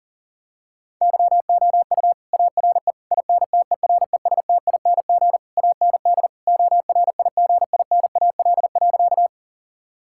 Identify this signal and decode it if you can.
Morse code, heard as a keyed tone, about 40 wpm, 700 Hz. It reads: YOU ARE INTERESTING AND ORIGINAL.
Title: Sentences from Top 1000 Words - Review Part 6 - 40wpm